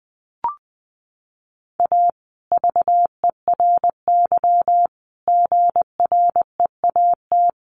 Morse code, heard as a keyed tone, about 20 wpm, 700 Hz. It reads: A VERY GREAT
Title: Sentences from Top 1400 Words 20wpm